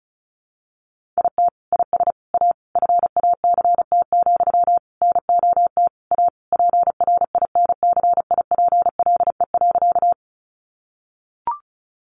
Morse code, heard as a keyed tone, about 34 wpm, 700 Hz. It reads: IT IS A FACT, NOT A PRINCIPLE.
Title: Sentences from Top 1400 Words 35wpm